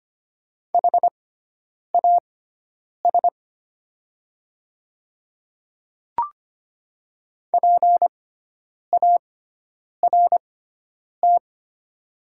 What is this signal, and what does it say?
Morse code, heard as a keyed tone, about 25 wpm, spaced as for 9 wpm, 700 Hz. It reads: HAS PART
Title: 6x Character Spacing - Top 200 Words - 25wpm